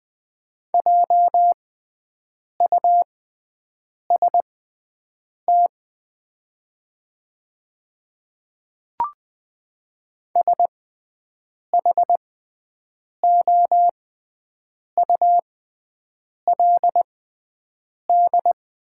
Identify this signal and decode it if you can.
Morse code, heard as a keyed tone, about 20 wpm, spaced as for 7 wpm, 700 Hz. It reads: JUST SHOULD
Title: 6x Character Spacing - Top 300 Words - 20wpm